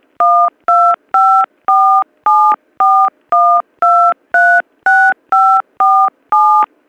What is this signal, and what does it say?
DTMF, heard as telephone touch tones, tones of about 310 ms. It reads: 1254741236547